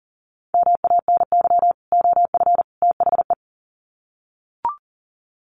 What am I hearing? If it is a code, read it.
Morse code, heard as a keyed tone, about 40 wpm, 700 Hz. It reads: MANY OF THE
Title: Sets of 3 Words 40wpm